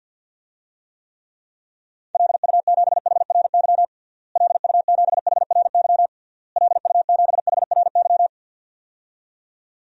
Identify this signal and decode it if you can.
Morse code, heard as a keyed tone, about 48 wpm, 700 Hz. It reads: LU7HRY LU7HRY LU7HRY
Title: DXCC-100 Call Signs - 6 Characters - Repeated 3x - 50wpm